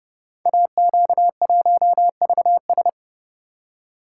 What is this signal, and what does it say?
Morse code, heard as a keyed tone, about 30 wpm, 700 Hz. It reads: AQ1VS
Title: Code Groups - 5 Letter:Numbers - 30wpm